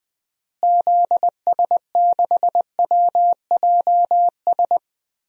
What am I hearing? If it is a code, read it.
Morse code, heard as a keyed tone, about 20 wpm, 700 Hz. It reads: ZS6WJS
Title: DXCC-100 Call Signs - 6 Characters - Repeated 3x - 20wpm